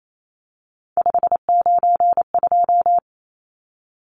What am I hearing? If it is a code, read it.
Morse code, heard as a keyed tone, about 28 wpm, 700 Hz. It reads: 592
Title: Numbers - 3-Digit 28wpm